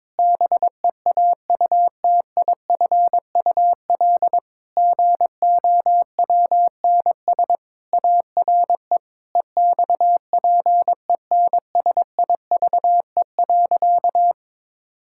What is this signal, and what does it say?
Morse code, heard as a keyed tone, about 22 wpm, 700 Hz. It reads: BEAUTIFUL GOWNS ARE EXPENSIVE.
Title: Pangram Challenge v2 - 22wpm